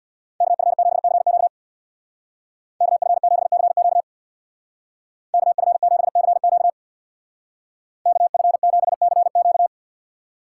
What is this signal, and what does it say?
Morse code, heard as a keyed tone, about 59 wpm, 700 Hz. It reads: KF7CQ KF7CQ KF7CQ KF7CQ
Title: Speed Racing - USA Call Signs 50wpm